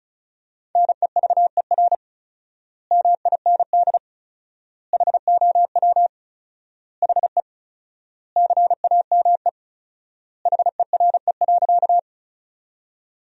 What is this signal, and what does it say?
Morse code, heard as a keyed tone, about 34 wpm, 700 Hz. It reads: NEVER MIND HOW HE CAME HERE.
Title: 4x Word-Spacing - Sentences from Top 500 Words - 35wpm